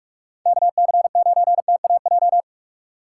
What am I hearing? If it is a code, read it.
Morse code, heard as a keyed tone, about 46 wpm, 700 Hz. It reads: KC9TAJ
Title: USA Call Signs - 6 Characters - 3x Character Spacing - 45wpm